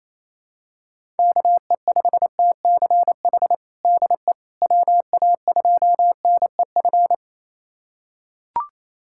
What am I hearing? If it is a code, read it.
Morse code, heard as a keyed tone, about 28 wpm, 700 Hz. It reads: KE5TCH DE WA2NEF
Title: USA Callsign DE Callsign 28wpm